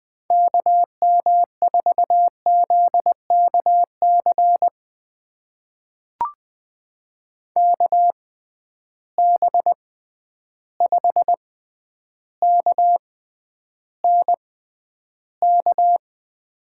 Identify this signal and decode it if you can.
Morse code, heard as a keyed tone, about 20 wpm, spaced as for 10 wpm, 700 Hz. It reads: KM4ZKC KB5KNK
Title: USA Call Signs - 6 Characters - 6x Character Spacing - 20wpm